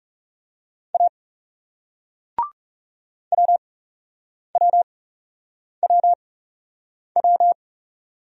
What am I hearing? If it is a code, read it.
Morse code, heard as a keyed tone, about 40 wpm, spaced as for 8 wpm, 700 Hz. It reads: A WWWW